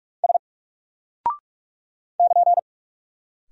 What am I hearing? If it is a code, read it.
Morse code, heard as a keyed tone, about 44 wpm, 700 Hz. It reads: S <KN>
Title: Single Character (Letter:Number:Punctuation:Prosign) Alt 45wpm